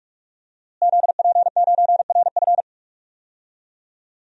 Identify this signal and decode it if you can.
Morse code, heard as a keyed tone, about 46 wpm, 700 Hz. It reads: ZP9RF
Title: DXCC-100 Call Signs - 5 Characters - Repeated 3x - 45wpm